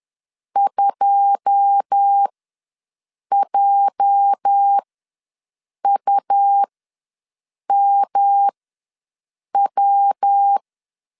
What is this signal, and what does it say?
Morse code, heard as a keyed tone, about 10 wpm, spaced as for 6 wpm, 800 Hz. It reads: 2JUMW
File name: LEZIONI / 9 2 ws 10wpm 4130